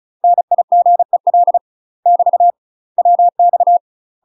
Morse code, heard as a keyed tone, about 36 wpm, 700 Hz. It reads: NIGEL <BT> WX
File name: UK MASTER QSOs TRAINING / 35WPM / UK MASTER QSOs 0035